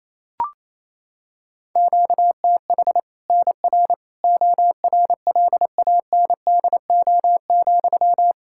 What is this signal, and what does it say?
Morse code, heard as a keyed tone, about 28 wpm, 700 Hz. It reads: QTH NR ORLANDO,
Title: Practice QSO - Exchange 1 - Beginning - 28wpm